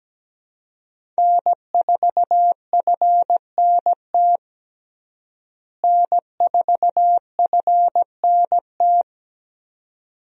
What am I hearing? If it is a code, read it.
Morse code, heard as a keyed tone, about 17 wpm, 700 Hz. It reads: N4FNT N4FNT
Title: USA Call Signs - 5 Characters - Repeated 2x - 17wpm